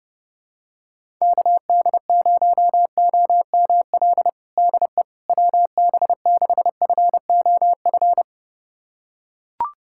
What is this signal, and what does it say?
Morse code, heard as a keyed tone, about 30 wpm, 700 Hz. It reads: KD0OML DE WB6FOF
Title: USA Callsign DE Callsign 30wpm